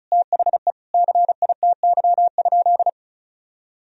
Morse code, heard as a keyed tone, about 36 wpm, 700 Hz. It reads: THE CITY?
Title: Sentences from Top 1000 Words - Review Part 2 - 35wpm